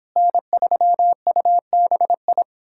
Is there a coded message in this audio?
Yes: Morse code from 0.2 to 2.4 s, heard as a keyed tone, about 29 wpm, 700 Hz. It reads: N3UBI